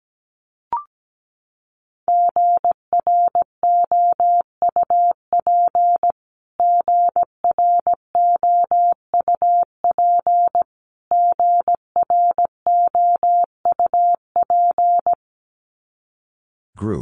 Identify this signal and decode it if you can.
Morse code, heard as a keyed tone, about 17 wpm, 700 Hz. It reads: GROUP GROUP GROUP
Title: Top 300 Words Repeated 17wpm